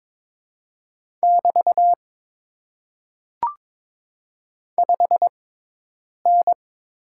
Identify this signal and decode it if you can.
Morse code, heard as a keyed tone, about 22 wpm, spaced as for 8 wpm, 700 Hz. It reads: <BT> 5N
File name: Top 90 QSO Elements - 6x Character Spacing - 22wpm